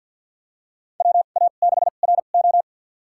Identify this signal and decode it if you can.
Morse code, heard as a keyed tone, about 53 wpm, spaced as for 37 wpm, 700 Hz. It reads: WA6RO
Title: USA Call Signs - 5 Characters - 2x Character Spacing - 50wpm